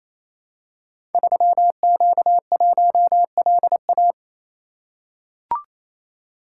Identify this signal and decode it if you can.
Morse code, heard as a keyed tone, about 29 wpm, 700 Hz. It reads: 3Q1LA
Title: Code Groups - 5 Letter:Numbers - 28wpm